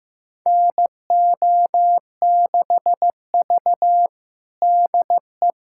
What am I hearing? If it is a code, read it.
Morse code, heard as a keyed tone, about 15 wpm, 700 Hz. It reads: NO6V DE
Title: USA Callsign DE Callsign - 4 Characters - 15wpm